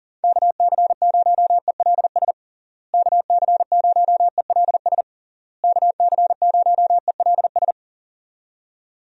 Morse code, heard as a keyed tone, about 40 wpm, 700 Hz. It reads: KC0ELS KC0ELS KC0ELS
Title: USA Call Signs - 6 Characters - Repeated 3x - 40wpm